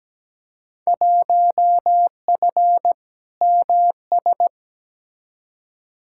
Morse code, heard as a keyed tone, about 17 wpm, 700 Hz. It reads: 1F MS
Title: Field Day Exchanges Encore - 17wpm